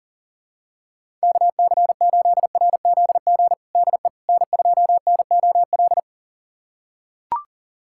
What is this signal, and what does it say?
Morse code, heard as a keyed tone, about 40 wpm, 700 Hz. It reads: KC8RZG DE N2NOL
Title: USA Callsign DE Callsign 40wpm